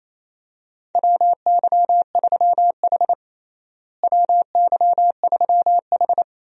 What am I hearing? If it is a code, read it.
Morse code, heard as a keyed tone, about 27 wpm, 700 Hz. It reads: WY3H WY3H
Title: USA Call Signs - 4 Characters - Repeated 2x - 28wpm